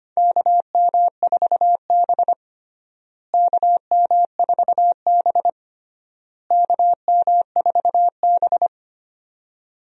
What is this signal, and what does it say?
Morse code, heard as a keyed tone, about 25 wpm, 700 Hz. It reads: KM4B KM4B KM4B
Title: USA Call Signs - 4 Characters - Repeated 3x - 25wpm